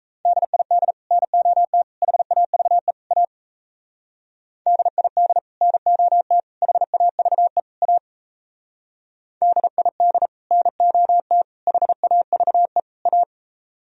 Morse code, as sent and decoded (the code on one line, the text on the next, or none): -.. .. -.. / -. --- - / .... .- ...- . / .- / -.. .. -.. / -. --- - / .... .- ...- . / .- / -.. .. -.. / -. --- - / .... .- ...- . / .-
DID NOT HAVE A DID NOT HAVE A DID NOT HAVE A